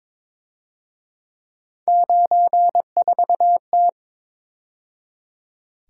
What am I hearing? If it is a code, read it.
Morse code, heard as a keyed tone, about 26 wpm, 700 Hz. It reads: 94T
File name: Speed-Racing - Three Letter-Number Combinations - 22wpm